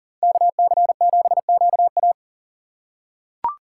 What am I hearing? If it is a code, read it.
Morse code, heard as a keyed tone, about 40 wpm, 700 Hz. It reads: KC7QA
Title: USA Call Signs - 5 Characters - 2x Character Spacing - 40wpm